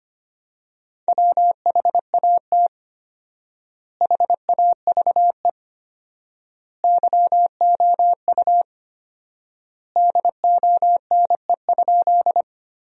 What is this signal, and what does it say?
Morse code, heard as a keyed tone, about 25 wpm, 700 Hz. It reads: WHAT HAVE YOU DONE?
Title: 4x Word-Spacing - Sentences from Top 500 Words - 25wpm